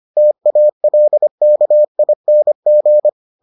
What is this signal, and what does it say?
Morse code, heard as a keyed tone, about 25 wpm, 600 Hz. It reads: TALKING